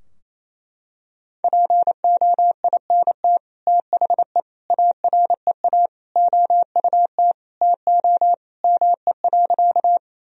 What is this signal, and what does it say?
Morse code, heard as a keyed tone, about 28 wpm, 700 Hz. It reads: POINT THE AREA OUT TO ME.
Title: Sentences from Top 1000 Words - Review Part 2 - 28wpm